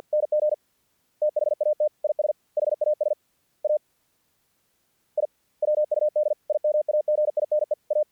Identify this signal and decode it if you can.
Morse code, heard as a keyed tone, about 50 wpm, 610 Hz. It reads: NG THAT IS HAS A I WUD IMAGINE A
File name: archived / CRSnet